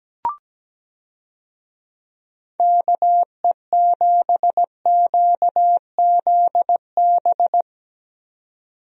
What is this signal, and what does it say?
Morse code, heard as a keyed tone, about 17 wpm, 700 Hz. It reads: KE7QZB
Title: USA Call Signs - 6 Characters - 17wpm